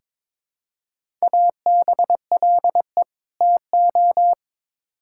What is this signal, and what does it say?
Morse code, heard as a keyed tone, about 22 wpm, 700 Hz. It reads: ABLE TO
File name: Build A Phrase - 2-Word Phrases - 22wpm